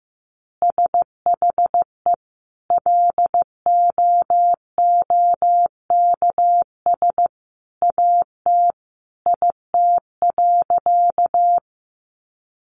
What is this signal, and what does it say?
Morse code, heard as a keyed tone, about 15 wpm, 700 Hz. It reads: SHE LOOKS AT IT.
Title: Additional Easy Phrases 15wpm